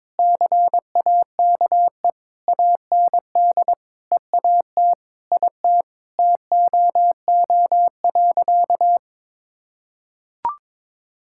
Morse code, heard as a keyed tone, about 21 wpm, 700 Hz. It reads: CAKE AND EAT IT TOO.
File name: Sentences with 9-Word Phrases - 22wpm